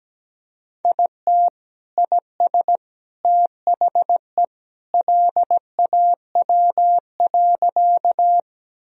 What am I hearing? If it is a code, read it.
Morse code, heard as a keyed tone, about 17 wpm, 700 Hz. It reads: IT IS THE LAW.
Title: Sentences from Top 1000 Words - Review Part 3 - 17wpm